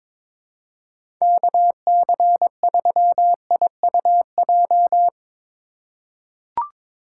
Morse code, heard as a keyed tone, about 22 wpm, 700 Hz. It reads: KC3IUJ
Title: USA Call Signs - 6 Characters - 2x Character Spacing - 22wpm